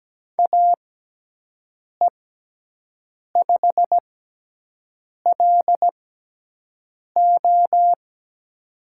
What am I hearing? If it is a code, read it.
Morse code, heard as a keyed tone, about 17 wpm, spaced as for 6 wpm, 700 Hz. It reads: AE5LO
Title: USA Call Signs - 5 Characters - 6x Character Spacing - 17wpm